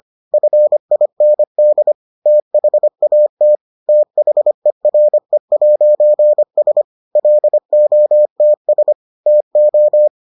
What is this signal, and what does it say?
Morse code, heard as a keyed tone, about 25 wpm, 600 Hz. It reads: FIND THAT THERE'S LOTS TO